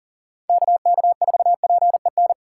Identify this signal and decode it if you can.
Morse code, heard as a keyed tone, about 40 wpm, 700 Hz. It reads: KK4PEN
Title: USA Call Signs - 6 Characters - 3x Character Spacing - 40wpm